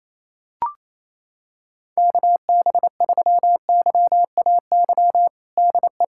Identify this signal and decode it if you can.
Morse code, heard as a keyed tone, about 28 wpm, 700 Hz. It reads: KB3YAY DE
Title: USA Callsign DE Callsign - 6 Characters - 28wpm